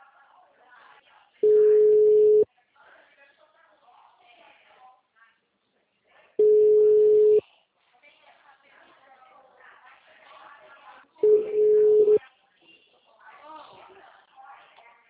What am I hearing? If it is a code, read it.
Ringback tone (Europe, 425 Hz), with speech.